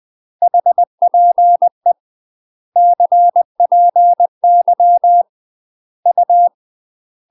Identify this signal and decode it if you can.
Morse code, heard as a keyed tone, about 20 wpm, 700 Hz. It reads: HPE CPY U
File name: UK MASTER QSOs TRAINING / 20WPM / UK MASTER QSOs 0071